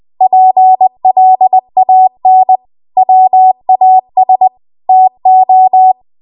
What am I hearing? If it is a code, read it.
Morse code, heard as a keyed tone, about 20 wpm, 750 Hz. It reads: PLAN WAS TO
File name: Morse / 20WPM